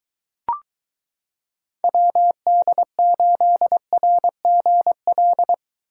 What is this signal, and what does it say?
Morse code, heard as a keyed tone, about 22 wpm, 700 Hz. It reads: WD8RGL